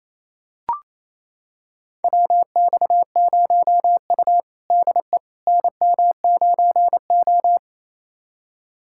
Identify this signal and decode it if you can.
Morse code, heard as a keyed tone, about 28 wpm, 700 Hz. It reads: WX0U DE NM9O
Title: USA Callsign DE Callsign - 4 Characters - 28wpm